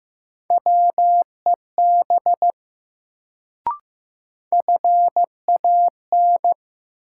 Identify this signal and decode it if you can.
Morse code, heard as a keyed tone, about 15 wpm, 700 Hz. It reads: WEB FAN
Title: Sentences from Top 1000 Words - Review Part 4 - New Word Drill - 15wpm